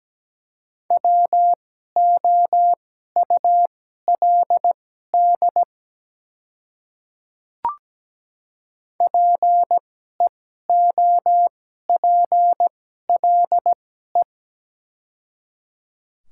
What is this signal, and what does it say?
Morse code, heard as a keyed tone, about 17 wpm, spaced as for 12 wpm, 700 Hz. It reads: WOULD PEOPLE
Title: 2x Character Spacing - Top 200 Words - 17wpm